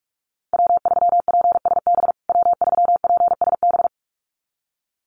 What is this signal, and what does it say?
Morse code, heard as a keyed tone, about 44 wpm, 700 Hz. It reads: W3PSB W3PSB
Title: USA Call Signs Repeated 2x 45wpm